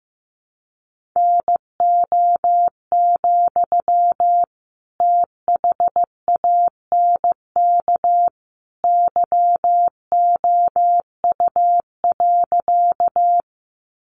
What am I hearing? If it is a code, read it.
Morse code, heard as a keyed tone, about 15 wpm, 700 Hz. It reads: NO, THANK YOU.
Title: Sentences from Top 600 Words 15wpm